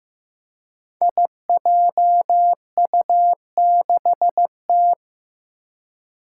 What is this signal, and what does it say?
Morse code, heard as a keyed tone, about 15 wpm, 700 Hz. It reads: IJU6T
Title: Five Challenging Letter-Number Combinations 15wpm